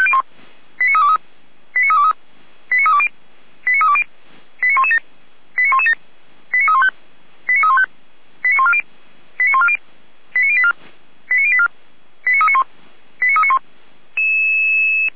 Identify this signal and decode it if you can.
Five-tone selective call (ZVEI), heard as quick runs of five tones, tones of about 70 ms. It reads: 89323 89323 89320 89320 89117 89117 89315 89315 89140 89140 89084 89084 89391 89391
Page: /455/SDR/ZVEI/